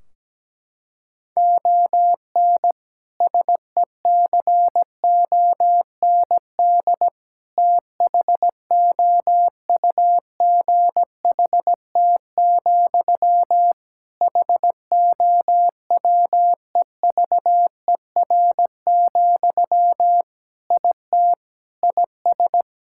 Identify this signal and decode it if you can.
Morse code, heard as a keyed tone, about 17 wpm, 700 Hz. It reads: ON SECOND THOUGHT, HOWEVER, IT IS
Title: Sentences from Top 1600 Words 17wpm